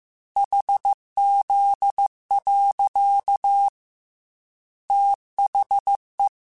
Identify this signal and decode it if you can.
Morse code, heard as a keyed tone, about 15 wpm, 800 Hz. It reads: HZ. THE
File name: Technical Subjects / Technical Subjects-mp3s / XL @13